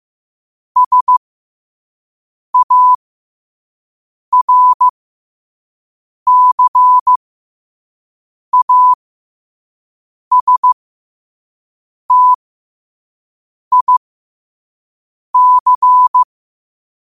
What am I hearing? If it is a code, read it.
Morse code, heard as a keyed tone, about 15 wpm, spaced as for 5 wpm, 1000 Hz. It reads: SARCASTIC
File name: qotdfiles / 5wpm